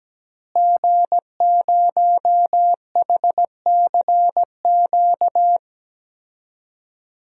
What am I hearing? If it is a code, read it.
Morse code, heard as a keyed tone, about 17 wpm, 700 Hz. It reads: G0HCQ
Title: DXCC-100 Call Signs - 5 Characters - Repeated 3x - 17wpm